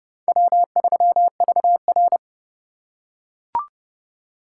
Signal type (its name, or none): Morse code